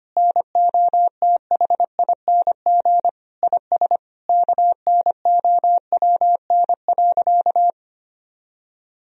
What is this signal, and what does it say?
Morse code, heard as a keyed tone, about 25 wpm, 700 Hz. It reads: NOTHING IS KNOWN.
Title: Sentences from Top 1000 Words - Review Part 4 - 25wpm